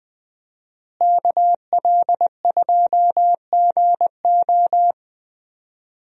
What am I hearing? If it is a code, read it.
Morse code, heard as a keyed tone, about 21 wpm, 700 Hz. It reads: KL2GO
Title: USA Call Signs - 5 Characters - Repeated 2x - 20wpm